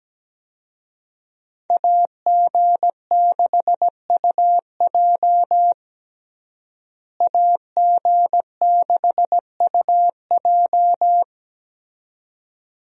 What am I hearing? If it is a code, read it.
Morse code, heard as a keyed tone, about 17 wpm, 700 Hz. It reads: AG6UJ AG6UJ
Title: USA Call Signs - 5 Characters - Repeated 2x - 17wpm